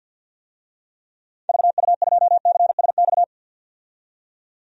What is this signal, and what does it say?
Morse code, heard as a keyed tone, about 50 wpm, 700 Hz. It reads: VU2CSX
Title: DXCC-100 Call Signs - 6 Characters - 50wpm